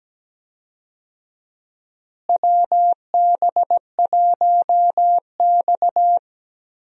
Morse code, heard as a keyed tone, about 17 wpm, 700 Hz. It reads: WB1X
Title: USA Call Signs - 4 Characters - Repeated 2x - 17wpm